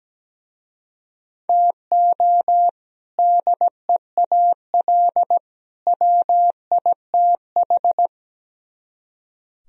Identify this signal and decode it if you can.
Morse code, heard as a keyed tone, about 18 wpm, 700 Hz. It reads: TO DEAL WITH
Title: Speed-Racing - Sets of 3 Words 17wpm